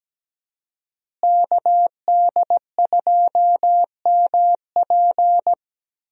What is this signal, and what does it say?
Morse code, heard as a keyed tone, about 17 wpm, 700 Hz. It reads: KD2MP
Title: USA Call Signs - 5 Characters - Repeated 2x - 17wpm